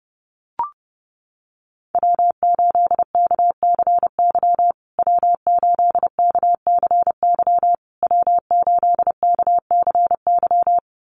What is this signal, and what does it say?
Morse code, heard as a keyed tone, about 30 wpm, 700 Hz. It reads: W8KCY W8KCY W8KCY
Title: USA Call Signs Repeated 3x 30wpm